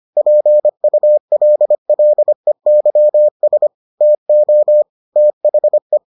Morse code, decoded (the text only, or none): PULLEYS TO THE